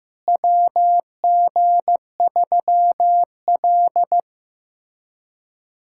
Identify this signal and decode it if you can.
Morse code, heard as a keyed tone, about 15 wpm, 700 Hz. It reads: WG3L
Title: USA Callsign DE Callsign - 4 Characters - 15wpm